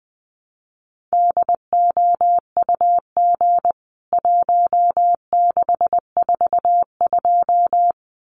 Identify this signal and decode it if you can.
Morse code, heard as a keyed tone, about 20 wpm, 700 Hz. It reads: DOUG 1642
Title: CWT Exchange 20wpm